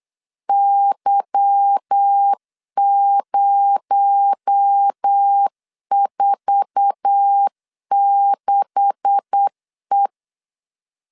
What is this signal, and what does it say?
Morse code, heard as a keyed tone, about 8 wpm, 800 Hz. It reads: Y046E
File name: LEZIONI / 9 286 8wpm 3425